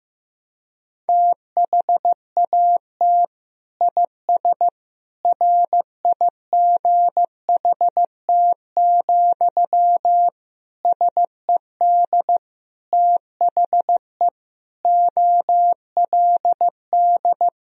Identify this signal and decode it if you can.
Morse code, heard as a keyed tone, about 15 wpm, 700 Hz. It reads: THAT IS RIGHT, SED THE OLD
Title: Sentences from Top 200 Words and Abbreviations 15wpm